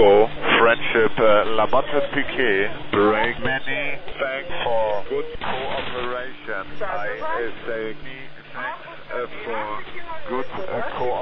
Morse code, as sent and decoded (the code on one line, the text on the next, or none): none